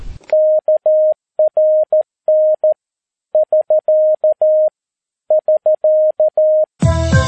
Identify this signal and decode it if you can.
Morse code, heard as a keyed tone, about 13 wpm, 620 Hz. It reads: KRN <SK> <SK>
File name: pi4aa / PI4AA Uitzending